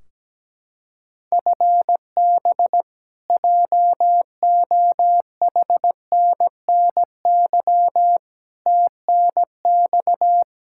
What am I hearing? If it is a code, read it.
Morse code, heard as a keyed tone, about 17 wpm, 700 Hz. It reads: FB JOHNNY TNX
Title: Practice QSO - Exchange 3 - More Info - 17wpm